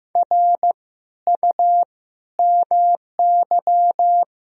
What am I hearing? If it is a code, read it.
Morse code, heard as a keyed tone, about 15 wpm, 700 Hz. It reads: R U MY
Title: Sentences from Top 300 Words and Abbreviations 15wpm